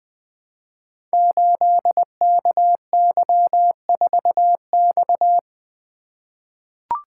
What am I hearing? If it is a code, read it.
Morse code, heard as a keyed tone, about 20 wpm, 700 Hz. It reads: 8KY4X
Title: Code Groups - 5 Letter:Numbers - 20wpm